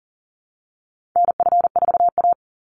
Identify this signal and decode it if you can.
Morse code, heard as a keyed tone, about 40 wpm, 700 Hz. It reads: NF4A
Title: Call Signs - 40wpm